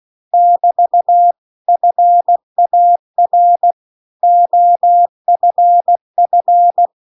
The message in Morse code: <BT> FAR OFF

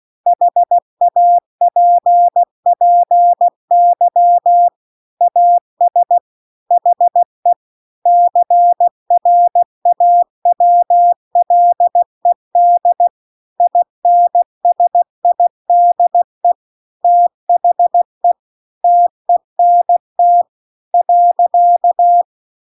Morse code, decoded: HAPPY AS HE CRAWLED INSIDE THE TENT .

